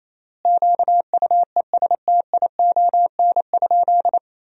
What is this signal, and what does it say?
Morse code, heard as a keyed tone, about 28 wpm, 700 Hz. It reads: QUESTION?
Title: 3x Word-Spacing - Sentences from Top 400 Words - 28wpm